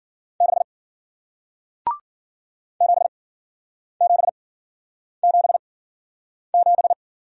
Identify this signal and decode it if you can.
Morse code, heard as a keyed tone, about 55 wpm, spaced as for 9 wpm, 700 Hz. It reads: 6 7777